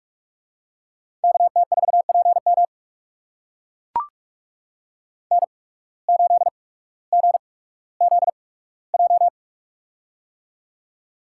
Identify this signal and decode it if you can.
Morse code, heard as a keyed tone, about 44 wpm, spaced as for 19 wpm, 700 Hz. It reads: KT4PM N8GZJ
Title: USA Call Signs - 5 Characters - 8x Character Spacing - 45wpm